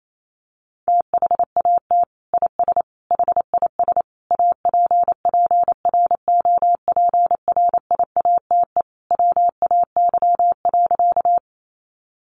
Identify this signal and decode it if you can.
Morse code, heard as a keyed tone, about 28 wpm, 690 Hz. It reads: THAT IS HIS APPROPRIATE WAY.
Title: Sentences from Top 900 Words 28wpm